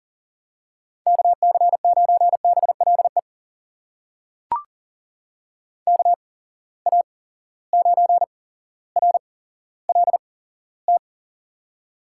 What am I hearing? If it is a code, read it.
Morse code, heard as a keyed tone, about 40 wpm, spaced as for 17 wpm, 700 Hz. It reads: KC9BLE KA9RLT
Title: USA Call Signs - 8x Character Spacing - 40wpm